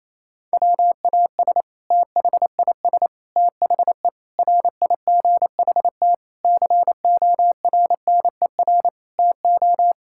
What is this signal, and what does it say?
Morse code, heard as a keyed tone, about 28 wpm, 700 Hz. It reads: WAS THIS THE RIGHT CORNER TO
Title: Sentences from Top 1600 Words 28wpm